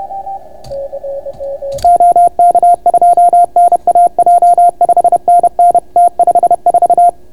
Morse code, heard as a keyed tone, about 31 wpm, 700 Hz. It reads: OK2NAJ 5NN T54